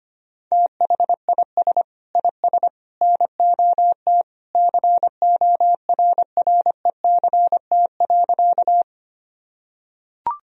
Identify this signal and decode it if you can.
Morse code, heard as a keyed tone, about 25 wpm, 700 Hz. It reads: THIS IS NOT CORRECT.
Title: Sentences from Top 1000 Words - Review Part 2 - 25wpm